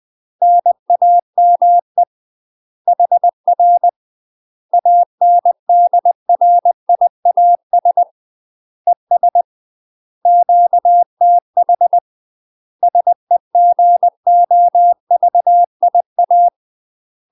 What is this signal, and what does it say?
Morse code, heard as a keyed tone, about 20 wpm, 700 Hz. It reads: NAME HR ANDRIAS ES QTH SEGOVIA
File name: UK MASTER QSOs TRAINING/20WPM